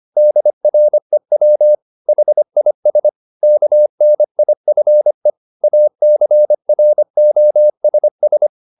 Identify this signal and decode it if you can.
Morse code, heard as a keyed tone, about 25 wpm, 600 Hz. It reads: DREW HIS KNIFE ACROSS
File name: the old man and the sea part 1 25wpm